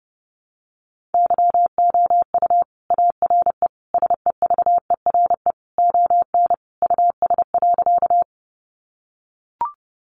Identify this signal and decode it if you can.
Morse code, heard as a keyed tone, about 30 wpm, 700 Hz. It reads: YOU ARE SEVERE ON US.
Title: Sentences from Top 1400 Words 30wpm